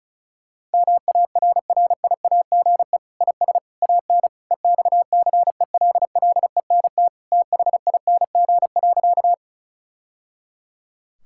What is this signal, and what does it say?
Morse code, heard as a keyed tone, about 34 wpm, 700 Hz. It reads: MARRIAGE IS AN EXCELLENT THING.
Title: Sentences from Top 1000 Words - Review Part 5 - 35wpm